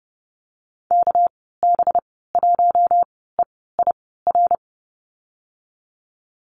Morse code, heard as a keyed tone, about 30 wpm, spaced as for 17 wpm, 700 Hz. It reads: KB1EIR
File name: Lesson 37g - Letter x - Call Signs 30wpm